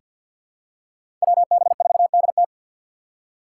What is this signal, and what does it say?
Morse code, heard as a keyed tone, about 50 wpm, 690 Hz. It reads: WB4DT
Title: USA Call Signs - 5 Characters - 2x Character Spacing - 50wpm